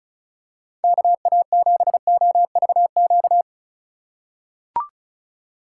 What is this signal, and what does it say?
Morse code, heard as a keyed tone, about 36 wpm, 700 Hz. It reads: KA7OVQ
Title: USA Call Signs - 6 Characters - 4x Character Spacing - 35wpm